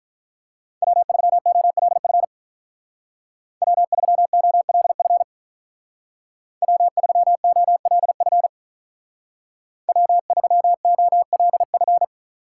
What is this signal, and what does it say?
Morse code, heard as a keyed tone, about 44 wpm, 700 Hz. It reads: W3OLF W3OLF W3OLF W3OLF
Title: Speed Racing - USA 1x3 Call Signs 35wpm